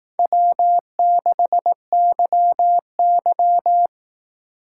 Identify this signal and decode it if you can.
Morse code, heard as a keyed tone, about 18 wpm, 700 Hz. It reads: W6YY
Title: Speed Racing - USA Call Signs 15wpm